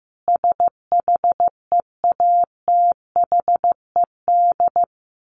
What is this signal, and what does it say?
Morse code, heard as a keyed tone, about 15 wpm, 700 Hz. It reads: SHEATHED
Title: A Sound Of Thunder 15wpm